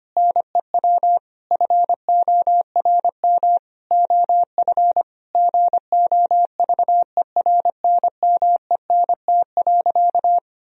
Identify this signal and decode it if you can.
Morse code, heard as a keyed tone, about 25 wpm, 700 Hz. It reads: NEW FORM OF GOVERNMENT.
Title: Sentences from Top 1000 Words - Review Part 3 - 25wpm